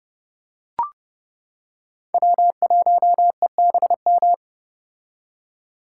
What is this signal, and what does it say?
Morse code, heard as a keyed tone, about 30 wpm, 700 Hz. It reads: W1EBM
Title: K1USN SST QSO Practice - 30wpm